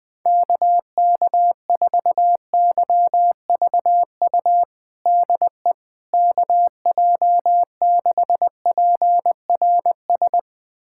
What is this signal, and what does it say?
Morse code, heard as a keyed tone, about 21 wpm, 700 Hz. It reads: KK4YVU DE KJ6PRS
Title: USA Callsign DE Callsign 20wpm